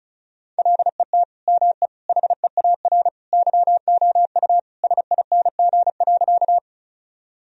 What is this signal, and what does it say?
Morse code, heard as a keyed tone, about 34 wpm, 700 Hz. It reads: LET ME HEAR YOU SING.
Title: Sentences from Top 1000 Words - Review Part 2 - 35wpm